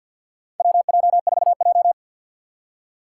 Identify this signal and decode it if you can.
Morse code, heard as a keyed tone, about 53 wpm, 700 Hz. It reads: WJ4J
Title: USA Call Signs - 4 Characters - 6x Character Spacing - 50wpm